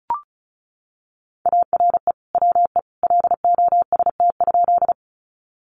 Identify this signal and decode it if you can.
Morse code, heard as a keyed tone, about 34 wpm, 700 Hz. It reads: ARE WE LOST?
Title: Sentences from Top 1400 Words 35wpm